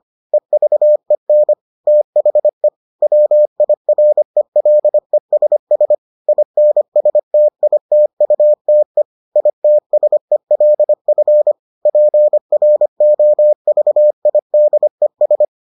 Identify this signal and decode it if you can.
Morse code, heard as a keyed tone, about 25 wpm, 600 Hz. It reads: EVEN THE WIRELESS INSTITUTE ITSELF PROVIDES